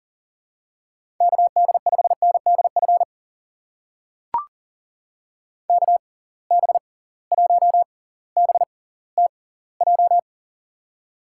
Morse code, heard as a keyed tone, about 40 wpm, spaced as for 21 wpm, 700 Hz. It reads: KD5NDF KB1BTJ
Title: USA Call Signs - 6 Characters - 6x Character Spacing - 40wpm